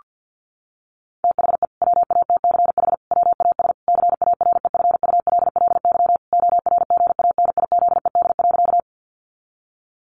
Morse code, heard as a keyed tone, about 50 wpm, 700 Hz. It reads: THE WATCH WAS CAREFULLY ORGANIZED.
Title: Sentences from Top 1300 Words 50wpm